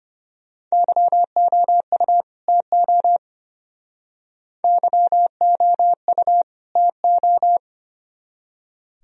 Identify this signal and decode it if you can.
Morse code, heard as a keyed tone, about 32 wpm, 700 Hz. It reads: YOU TO YOU TO